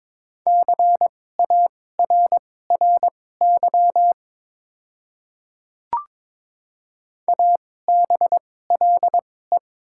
Morse code, heard as a keyed tone, about 22 wpm, spaced as for 16 wpm, 690 Hz. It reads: CARRY ABLE
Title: 2x Character Spacing - Top 500 Words - 22wpm